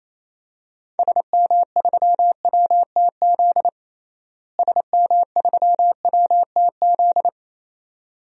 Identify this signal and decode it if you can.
Morse code, heard as a keyed tone, about 28 wpm, 700 Hz. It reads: SM3WTZ SM3WTZ
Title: DXCC-100 Call Signs - 6 Characters - Repeated 2x - 28wpm